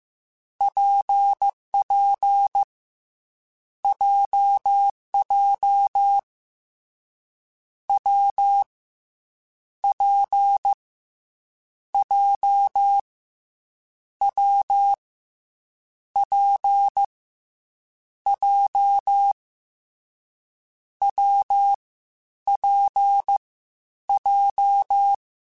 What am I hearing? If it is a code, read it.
Morse code, heard as a keyed tone, about 15 wpm, 800 Hz. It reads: PP JJ W P J W P J W P J
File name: Difficult Combos / Difficult Combos-mp3s / WPJ WPJ1 @13